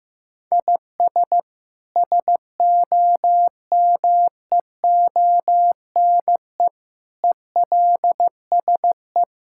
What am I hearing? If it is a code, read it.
Morse code, heard as a keyed tone, about 15 wpm, 700 Hz. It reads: IS SOMEONE ELSE
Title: Sentences from Top 1000 Words - Review Part 5 - 15wpm